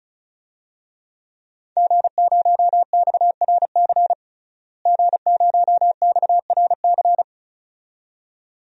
Morse code, heard as a keyed tone, about 36 wpm, 700 Hz. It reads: G0XRC G0XRC
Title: DXCC-100 Call Signs - 5 Characters - Repeated 2x - 35wpm